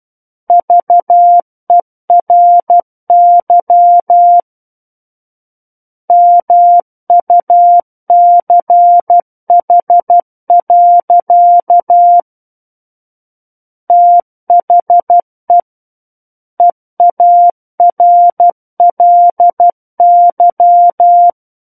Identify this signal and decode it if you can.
Morse code, heard as a keyed tone, about 12 wpm, 700 Hz. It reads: VERY MUCH. THE EARLY
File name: Book One / 03-Chapter 3 - 12 WPM